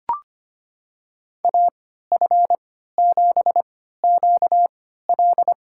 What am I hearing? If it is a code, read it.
Morse code, heard as a keyed tone, about 25 wpm, spaced as for 14 wpm, 700 Hz. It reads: AF7QL
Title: USA Call Signs - 5 Characters - 3x Character Spacing - 25wpm